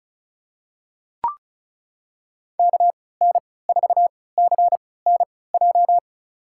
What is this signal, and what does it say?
Morse code, heard as a keyed tone, about 36 wpm, spaced as for 20 wpm, 700 Hz. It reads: KN4CNJ
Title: USA Call Signs - 6 Characters - 3x Character Spacing - 35wpm